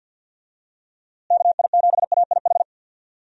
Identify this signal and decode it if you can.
Morse code, heard as a keyed tone, about 47 wpm, 700 Hz. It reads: KI7AIH
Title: USA Call Signs - 6 Characters - Repeated 3x - 50wpm